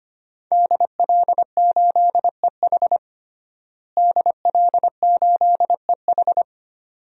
Morse code, heard as a keyed tone, about 25 wpm, 700 Hz. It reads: DL8EH DL8EH
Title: DXCC-100 Call Signs - 5 Characters - Repeated 2x - 25wpm